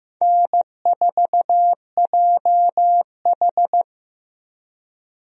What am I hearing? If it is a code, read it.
Morse code, heard as a keyed tone, about 15 wpm, 700 Hz. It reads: N4JH
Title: USA Call Signs - 4 Characters - Repeated 3x - 15wpm